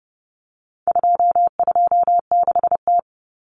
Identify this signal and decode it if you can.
Morse code, heard as a keyed tone, about 30 wpm, 700 Hz. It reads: 226T
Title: Common Cut Numbers - 4-Digit 30wpm